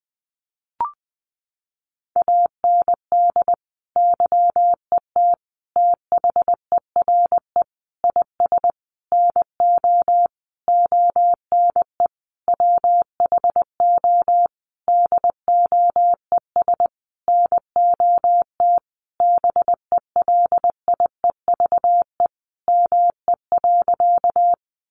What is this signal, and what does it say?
Morse code, heard as a keyed tone, about 20 wpm, 700 Hz. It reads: AND YET THERE IS NO ONE WHO DOES NOT BELIEVE ME.
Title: Sentences from Top 1000 Words 20wpm